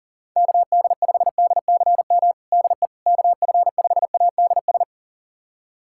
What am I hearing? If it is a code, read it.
Morse code, heard as a keyed tone, about 40 wpm, 700 Hz. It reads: KD5DCM DE KF5ADS
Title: USA Callsign DE Callsign 40wpm